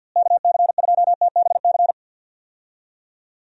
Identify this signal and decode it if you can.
Morse code, heard as a keyed tone, about 47 wpm, 700 Hz. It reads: KC2TBC
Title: USA Call Signs - 6 Characters - Repeated 3x - 50wpm